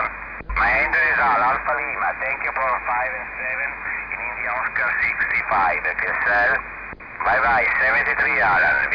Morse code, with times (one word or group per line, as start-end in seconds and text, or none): none